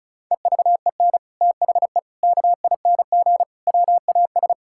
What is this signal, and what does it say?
Morse code, heard as a keyed tone, about 36 wpm, 700 Hz. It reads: EVEN THE KING WAS
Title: Sentences from Top 1000 Words - Review Part 6 - 35wpm